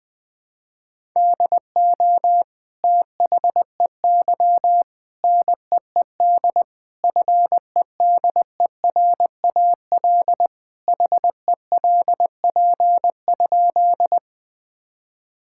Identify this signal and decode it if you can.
Morse code, heard as a keyed tone, about 20 wpm, 700 Hz. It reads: DO THEY NEED FEDERAL HELP?
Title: Sentences from Top 1000 Words - Review Part 5 - 20wpm